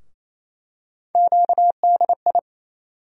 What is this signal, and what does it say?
Morse code, heard as a keyed tone, about 28 wpm, 700 Hz. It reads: QDI
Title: USA Call Sign Postfixes - 3 Letters 28wpm